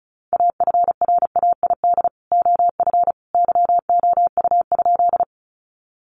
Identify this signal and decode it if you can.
Morse code, heard as a keyed tone, about 34 wpm, 700 Hz. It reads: AFRAID OF YOU?
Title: Sentences from Top 1500 Words 35wpm